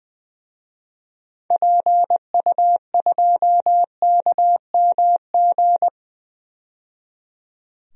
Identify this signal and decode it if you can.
Morse code, heard as a keyed tone, about 20 wpm, 700 Hz. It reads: PU2KMG
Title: DXCC-100 Call Signs - 6 Characters - 20wpm